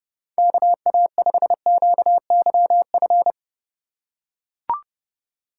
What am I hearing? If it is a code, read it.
Morse code, heard as a keyed tone, about 30 wpm, 700 Hz. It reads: KA5QYF
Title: USA Call Signs - 6 Characters - 2x Character Spacing - 30wpm